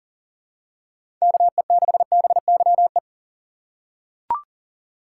Morse code, heard as a keyed tone, about 40 wpm, 700 Hz. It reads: KE6BYE
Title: USA Call Signs - 6 Characters - 3x Character Spacing - 40wpm